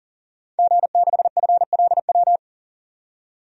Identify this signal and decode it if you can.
Morse code, heard as a keyed tone, about 40 wpm, 700 Hz. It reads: G6FLW
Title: DXCC-100 Call Signs - 5 Characters - 40wpm